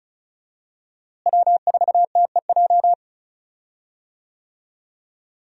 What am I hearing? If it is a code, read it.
Morse code, heard as a keyed tone, about 34 wpm, 700 Hz. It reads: W4TEJ
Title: USA Call Signs - 5 Characters - Repeated 3x - 35wpm